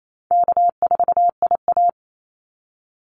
Morse code, heard as a keyed tone, about 28 wpm, 700 Hz. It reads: K4IA
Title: Easy Call Signs Repeated 2x - 28wpm